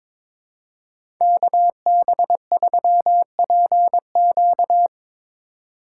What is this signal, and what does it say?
Morse code, heard as a keyed tone, about 22 wpm, 700 Hz. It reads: KB3PQ
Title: USA Call Signs - 5 Characters - 2x Character Spacing - 22wpm